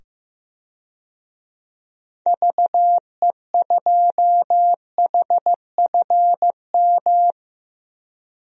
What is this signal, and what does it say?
Morse code, heard as a keyed tone, about 15 wpm, 700 Hz. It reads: VE2HFM
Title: DXCC-100 Call Signs - 6 Characters - 15wpm